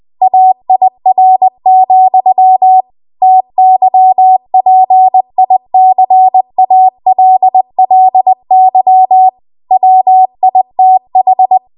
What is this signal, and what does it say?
Morse code, heard as a keyed tone, about 20 wpm, 750 Hz. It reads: AIR, TYPICALLY WITH
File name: Archive / 20 WPM / 20WPM